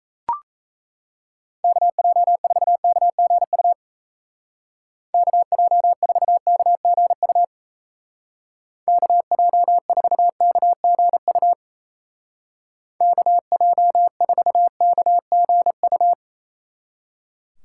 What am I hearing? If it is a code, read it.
Morse code, heard as a keyed tone, about 34 wpm, 700 Hz. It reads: KJ4KGU KJ4KGU KJ4KGU KJ4KGU
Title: Speed Racing - USA Call Signs 28wpm